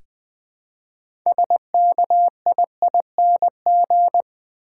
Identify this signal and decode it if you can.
Morse code, heard as a keyed tone, about 20 wpm, 700 Hz. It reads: SKIING
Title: Build A Word - 2 Letters At A Time - 6-Letter Words - 20wpm